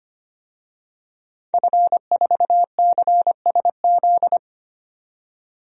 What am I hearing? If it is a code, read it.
Morse code, heard as a keyed tone, about 25 wpm, 700 Hz. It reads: F4CSZ
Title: DXCC-100 Call Signs - 5 Characters - 25wpm